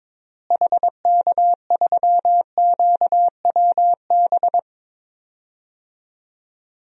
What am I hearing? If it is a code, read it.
Morse code, heard as a keyed tone, about 22 wpm, 700 Hz. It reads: HK3QWB
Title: DXCC-100 Call Signs - 6 Characters - Repeated 2x - 22wpm